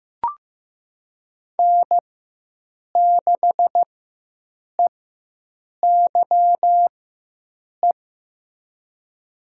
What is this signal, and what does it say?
Morse code, heard as a keyed tone, about 15 wpm, spaced as for 7 wpm, 700 Hz. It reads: N6EYE
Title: USA Call Signs - 5 Characters - 4x Character Spacing - 15wpm